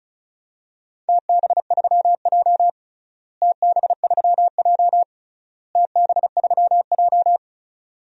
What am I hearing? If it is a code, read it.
Morse code, heard as a keyed tone, about 36 wpm, 700 Hz. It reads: TB3J TB3J TB3J
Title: DXCC-100 Call Signs - 4 Characters - Repeated 3x - 35wpm